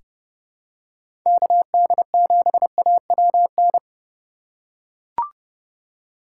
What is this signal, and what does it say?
Morse code, heard as a keyed tone, about 32 wpm, 700 Hz. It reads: KD7AWN
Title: USA Call Signs - 6 Characters - 3x Character Spacing - 30wpm